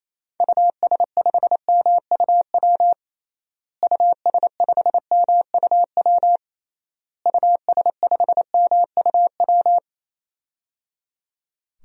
Morse code, heard as a keyed tone, about 28 wpm, 700 Hz. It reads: US5MUW US5MUW US5MUW
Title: DXCC-100 Call Signs - 6 Characters - Repeated 3x - 28wpm